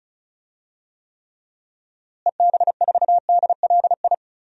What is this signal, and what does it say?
Morse code, heard as a keyed tone, about 36 wpm, 700 Hz. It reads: EB4DLI
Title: DXCC-100 Call Signs - 6 Characters - Repeated 2x - 35wpm